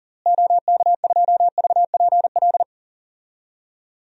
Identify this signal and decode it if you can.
Morse code, heard as a keyed tone, about 40 wpm, 700 Hz. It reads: OK2VPL
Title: DXCC-100 Call Signs - 6 Characters - Repeated 2x - 40wpm